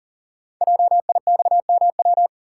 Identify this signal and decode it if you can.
Morse code, heard as a keyed tone, about 39 wpm, 700 Hz. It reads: JIXMW
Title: Code Groups - 5 Letters - 40wpm